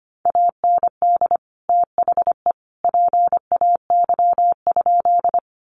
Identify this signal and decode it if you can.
Morse code, heard as a keyed tone, about 25 wpm, 700 Hz. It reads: AND THE PAY?
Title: Sentences from Top 700 Words 25wpm